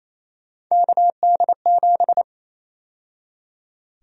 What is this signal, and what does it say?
Morse code, heard as a keyed tone, about 28 wpm, 700 Hz. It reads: KD7
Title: USA Call Sign Prefixes - 28wpm